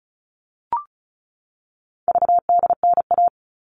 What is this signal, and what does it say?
Morse code, heard as a keyed tone, about 34 wpm, 700 Hz. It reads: VDNA
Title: All Cut Numbers - 4-Digit 35wpm